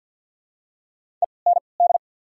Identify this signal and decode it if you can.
Morse code, heard as a keyed tone, about 53 wpm, spaced as for 29 wpm, 700 Hz. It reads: END
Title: 3x Character Spacing - Top 300 Words - 50wpm